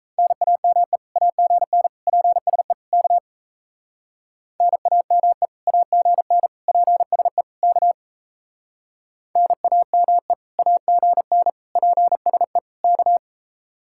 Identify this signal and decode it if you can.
Morse code, heard as a keyed tone, about 36 wpm, 700 Hz. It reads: NAME AGN PSE K NAME AGN PSE K NAME AGN PSE K